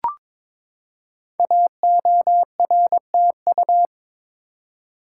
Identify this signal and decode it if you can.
Morse code, heard as a keyed tone, about 23 wpm, 700 Hz. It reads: AORTU
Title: Code Groups - 5 Letters - 22wpm